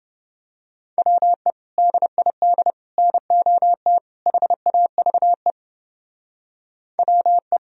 Morse code, heard as a keyed tone, about 29 wpm, 700 Hz. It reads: WE DID NOT HAVE WE